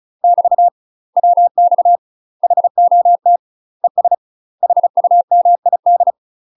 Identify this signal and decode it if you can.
Morse code, heard as a keyed tone, about 34 wpm, 700 Hz. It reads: <BT> WX HOT ES HUMID
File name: UK MASTER QSOs TRAINING / 35WPM / UK MASTER QSOs 0025